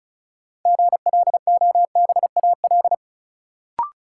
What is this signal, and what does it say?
Morse code, heard as a keyed tone, about 34 wpm, 700 Hz. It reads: GLOBAL
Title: Sentences from Top 1200 Words - New Word Drill - 35wpm